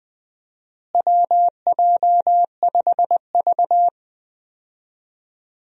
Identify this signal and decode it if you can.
Morse code, heard as a keyed tone, about 20 wpm, 700 Hz. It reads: WJ5V
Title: USA Call Signs - 4 Characters - Repeated 2x - 20wpm